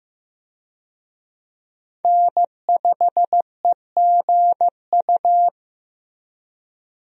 Morse code, heard as a keyed tone, about 15 wpm, 700 Hz. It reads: N5EGU
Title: USA Call Signs - 5 Characters - Repeated 3x - 15wpm